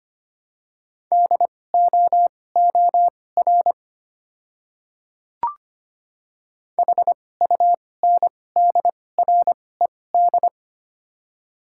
Morse code, heard as a keyed tone, about 25 wpm, spaced as for 18 wpm, 690 Hz. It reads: DOOR HUNDRED
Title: 2x Character Spacing - Top 500 Words - 25wpm